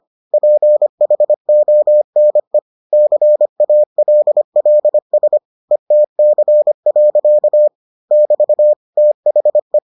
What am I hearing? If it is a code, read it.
Morse code, heard as a keyed tone, about 25 wpm, 600 Hz. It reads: PHONE CALLS ETC. <BT> THE